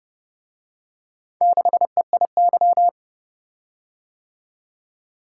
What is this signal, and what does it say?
Morse code, heard as a keyed tone, about 27 wpm, 700 Hz. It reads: 6EIY